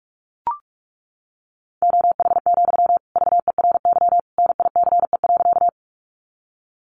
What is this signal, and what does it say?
Morse code, heard as a keyed tone, about 44 wpm, 700 Hz. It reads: OH, VERY NICE.
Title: Sentences from Top 1100 Words 45wpm